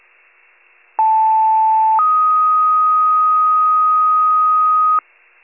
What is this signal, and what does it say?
Two-tone page, heard as a one-second tone, then a three-second tone, about 880 then 1250 Hz. It is heard from 1.0 to 5.0 s.
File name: wcalert